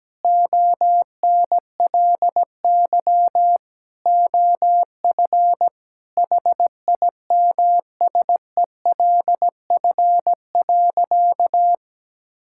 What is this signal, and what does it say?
Morse code, heard as a keyed tone, about 17 wpm, 700 Hz. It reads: ONLY OF HIMSELF.
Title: Sentences from Top 1600 Words 17wpm